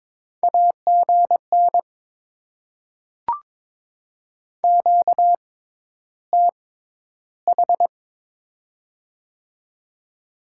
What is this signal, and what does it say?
Morse code, heard as a keyed tone, about 23 wpm, spaced as for 11 wpm, 690 Hz. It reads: AGN QTH
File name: Top 20 QSO Elements - 6x Character Spacing - 22wpm